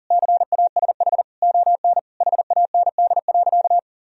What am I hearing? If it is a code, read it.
Morse code, heard as a keyed tone, about 40 wpm, 700 Hz. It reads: CASH ON HAND.
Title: Sentences from Top 1000 Words - Review Part 4 - 40wpm